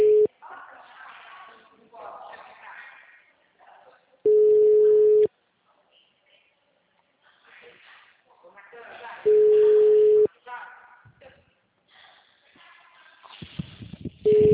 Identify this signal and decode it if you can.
Ringback tone (Europe, 425 Hz), with speech.